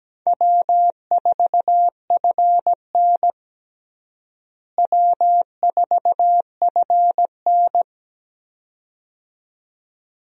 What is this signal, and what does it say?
Morse code, heard as a keyed tone, about 17 wpm, 700 Hz. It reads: W4FN W4FN
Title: USA Call Signs - 4 Characters - Repeated 2x - 17wpm